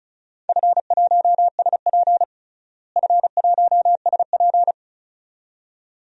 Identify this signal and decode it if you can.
Morse code, heard as a keyed tone, about 36 wpm, 700 Hz. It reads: F1SP F1SP
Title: DXCC-100 Call Signs - 4 Characters - Repeated 2x - 35wpm